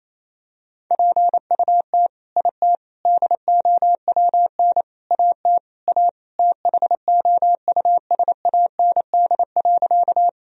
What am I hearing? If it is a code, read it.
Morse code, heard as a keyed tone, about 28 wpm, 700 Hz. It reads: PUT IT DOWN AT A THOUSAND.
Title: Sentences from Top 1000 Words - Review Part 3 - 28wpm